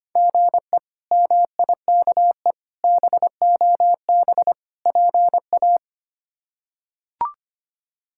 Morse code, heard as a keyed tone, about 25 wpm, 700 Hz. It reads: GE MIKE BOB PA
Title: K1USN SST QSO Practice - 25wpm